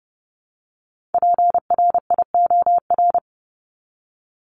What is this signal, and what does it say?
Morse code, heard as a keyed tone, about 30 wpm, 700 Hz. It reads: PRIOR
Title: Top 1501-1750 Words 30wpm